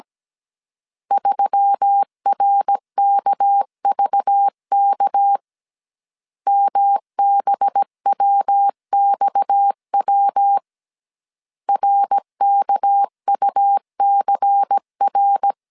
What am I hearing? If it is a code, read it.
Morse code, heard as a keyed tone, about 17 wpm, 800 Hz. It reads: 3RKVK MBWXW RKUCR
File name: CW / DET / 1 6 1500C S A 16wpm 2635